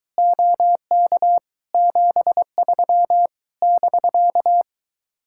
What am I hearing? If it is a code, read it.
Morse code, heard as a keyed tone, about 23 wpm, 700 Hz. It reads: OK 73 <BK>